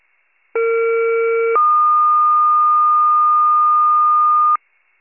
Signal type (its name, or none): two-tone page